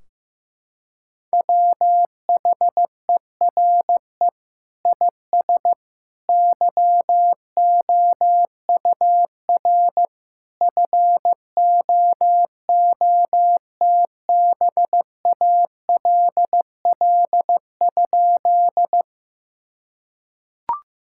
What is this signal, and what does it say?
Morse code, heard as a keyed tone, about 16 wpm, 700 Hz. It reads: WHERE IS YOUR FOOTBALL?
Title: Sentences from Top 1600 Words 15wpm